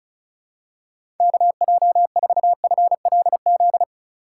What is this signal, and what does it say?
Morse code, heard as a keyed tone, about 36 wpm, 700 Hz. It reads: KJ4FLZ
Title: USA Call Signs - 4x Character Spacing - 35wpm